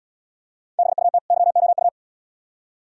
Morse code, heard as a keyed tone, about 76 wpm, 700 Hz. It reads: BUT YOU